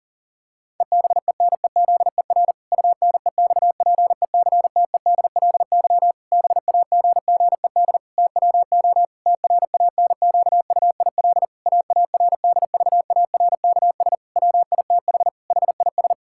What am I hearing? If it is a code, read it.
Morse code, heard as a keyed tone, about 40 wpm, 700 Hz. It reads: EBENEZER UNEXPECTEDLY BAGGED TWO TRANQUIL AARDVARKS WITH HIS
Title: Pangram Challenge v2 - 40wpm